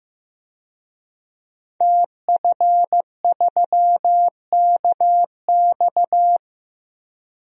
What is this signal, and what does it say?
Morse code, heard as a keyed tone, about 15 wpm, 700 Hz. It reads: TF3KX
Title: DXCC-100 Call Signs - 5 Characters - Repeated 2x - 15wpm